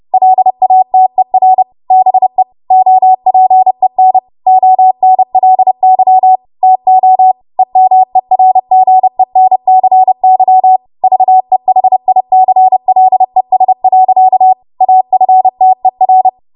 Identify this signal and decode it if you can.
Morse code, heard as a keyed tone, about 30 wpm, 750 Hz. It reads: LATER BE OPEN ONLY TO EMERGENCY VEHICLES. AFTER